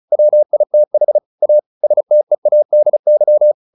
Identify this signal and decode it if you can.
Morse code, heard as a keyed tone, about 34 wpm, 600 Hz. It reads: WITH A STEADY
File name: the old man and the sea / the old man and the sea 1 35wpm